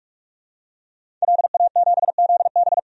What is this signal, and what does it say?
Morse code, heard as a keyed tone, about 46 wpm, 700 Hz. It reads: LA7ZB
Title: DXCC-100 Call Signs - 5 Characters - 45wpm